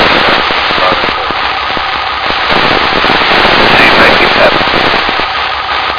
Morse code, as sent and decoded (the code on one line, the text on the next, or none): none